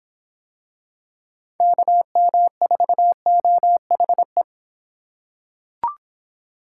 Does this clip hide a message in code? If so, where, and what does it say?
Yes: Morse code from 1.6 to 4.4 s, heard as a keyed tone, about 23 wpm, 700 Hz. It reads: KM4OHE